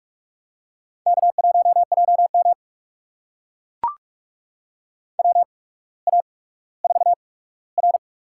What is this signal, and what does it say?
Morse code, heard as a keyed tone, about 45 wpm, spaced as for 19 wpm, 700 Hz. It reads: K1JM WA4R
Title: USA Call Signs - 8x Character Spacing - 45wpm